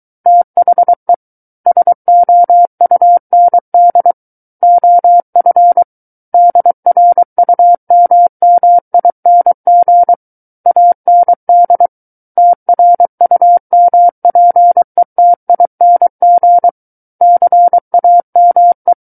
Morse code, heard as a keyed tone, about 23 wpm, 700 Hz. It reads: THE SOUND OF DRUMMING AND TRUMPETING CAME
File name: Book One / 14-Chapter 14 - 23 WPM